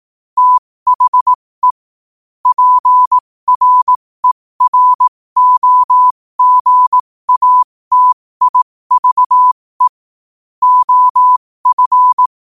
Morse code, heard as a keyed tone, about 18 wpm, spaced as for 15 wpm, 1000 Hz. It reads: THE PREROGATIVE OF
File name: 15wpm